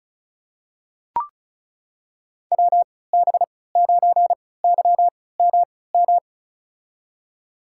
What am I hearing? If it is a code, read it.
Morse code, heard as a keyed tone, about 36 wpm, spaced as for 20 wpm, 700 Hz. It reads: WB9YMM
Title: USA Call Signs - 6 Characters - 3x Character Spacing - 35wpm